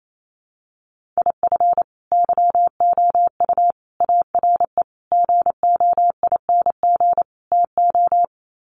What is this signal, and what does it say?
Morse code, heard as a keyed tone, about 28 wpm, 700 Hz. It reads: IF YOU ARE GOING TO
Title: Sets of 5 Words Encore2 28wpm